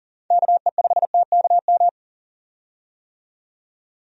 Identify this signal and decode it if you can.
Morse code, heard as a keyed tone, about 40 wpm, 700 Hz. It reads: KE5TKM
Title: USA Call Signs - 6 Characters - 40wpm